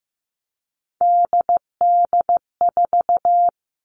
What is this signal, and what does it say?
Morse code, heard as a keyed tone, about 15 wpm, 700 Hz. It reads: DD4
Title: All Cut Numbers - 3-Digit 15wpm